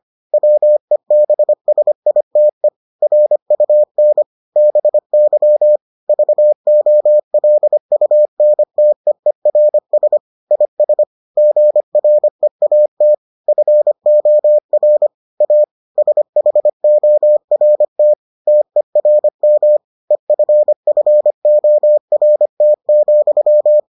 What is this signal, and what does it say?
Morse code, heard as a keyed tone, about 25 wpm, 600 Hz. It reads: WEBSITE RUN BY VOLUNTEERS IS GREAT FOR A SHORT TERM EFFORT,